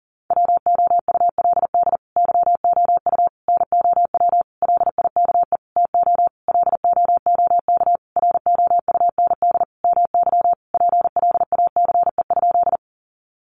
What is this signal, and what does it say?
Morse code, heard as a keyed tone, about 40 wpm, 700 Hz. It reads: WOULD YOU NOW LIKE TO LOOK ROUND MY PLACE?
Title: Sentences from Top 200 Words 40wpm